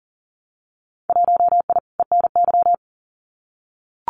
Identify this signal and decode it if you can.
Morse code, heard as a keyed tone, about 41 wpm, 700 Hz. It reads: 1I ENY
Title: Winter Field Day Exchanges Encore 40wpm